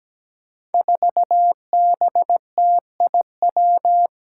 Morse code, heard as a keyed tone, about 17 wpm, 700 Hz. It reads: 4BTIW
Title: Five Challenging Letter-Number Combinations 17wpm